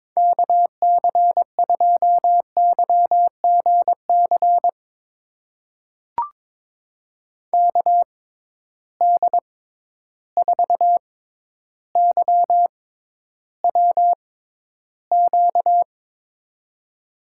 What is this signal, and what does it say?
Morse code, heard as a keyed tone, about 22 wpm, spaced as for 11 wpm, 700 Hz. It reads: KC2YGC KD4YWQ
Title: USA Call Signs - 6 Characters - 6x Character Spacing - 22wpm